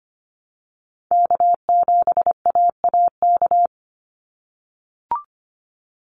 Morse code, heard as a keyed tone, about 25 wpm, 700 Hz. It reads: K7AAK
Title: USA Call Signs 25wpm